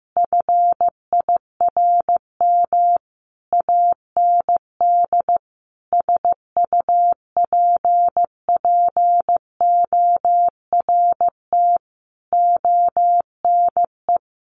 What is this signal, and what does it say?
Morse code, heard as a keyed tone, about 15 wpm, 700 Hz. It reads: FIRM AND SUPPORT ONE
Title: Sentences from Top 1400 Words 15wpm